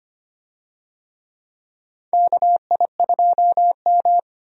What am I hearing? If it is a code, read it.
Morse code, heard as a keyed tone, about 25 wpm, 700 Hz. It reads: KI2M
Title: USA Call Signs - 4 Characters - 25wpm